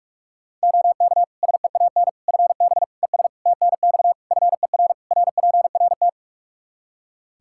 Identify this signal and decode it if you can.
Morse code, heard as a keyed tone, about 46 wpm, 700 Hz. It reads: OK SEAN FB ES TNX FER RPRT
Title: Practice QSO - Exchange 2 - Info - 45wpm